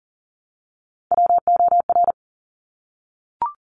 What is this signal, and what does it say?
Morse code, heard as a keyed tone, about 40 wpm, 700 Hz. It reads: WOR
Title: Top Three Letter Combinations in Words 40wpm